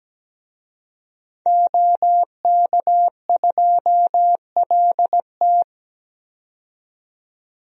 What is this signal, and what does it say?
Morse code, heard as a keyed tone, about 17 wpm, 700 Hz. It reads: OK2LT
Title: DXCC-100 Call Signs - 5 Characters - 17wpm